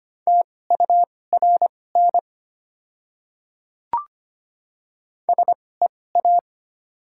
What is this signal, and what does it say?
Morse code, heard as a keyed tone, about 25 wpm, spaced as for 18 wpm, 700 Hz. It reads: TURN SEA
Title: 2x Character Spacing - Top 500 Words - 25wpm